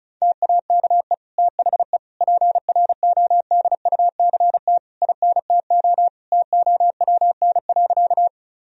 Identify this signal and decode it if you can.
Morse code, heard as a keyed tone, about 34 wpm, 700 Hz. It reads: TAKE THE PRODUCT INTO TOWN.
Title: Sentences from Top 1000 Words - Review Part 2 - 35wpm